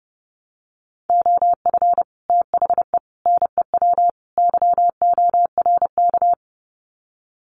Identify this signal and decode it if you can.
Morse code, heard as a keyed tone, about 30 wpm, 700 Hz. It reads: OF THE NEW YORK
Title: Sets of 4 Words 30wpm